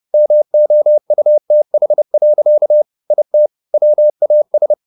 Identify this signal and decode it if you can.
Morse code, heard as a keyed tone, about 30 wpm, 600 Hz. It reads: MOUTH. IT WAS